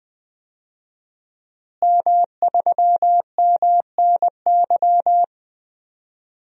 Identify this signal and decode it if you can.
Morse code, heard as a keyed tone, about 20 wpm, 700 Hz. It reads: M3MNY
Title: DXCC-100 Call Signs - 5 Characters - Repeated 3x - 20wpm